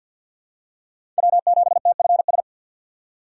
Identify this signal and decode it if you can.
Morse code, heard as a keyed tone, about 53 wpm, 700 Hz. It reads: W7TFS
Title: USA Call Signs - 5 Characters - 3x Character Spacing - 50wpm